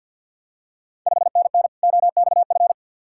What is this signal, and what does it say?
Morse code, heard as a keyed tone, about 51 wpm, 690 Hz. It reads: 5NN OXF
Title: Ontario QSO Party Exchange 50wpm